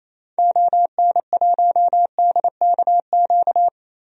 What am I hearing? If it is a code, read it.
Morse code, heard as a keyed tone, about 28 wpm, 700 Hz. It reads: ON1DKQ
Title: DXCC-100 Call Signs - 6 Characters - 28wpm